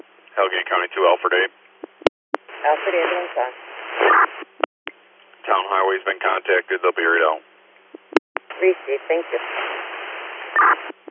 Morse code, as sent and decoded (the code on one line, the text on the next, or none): none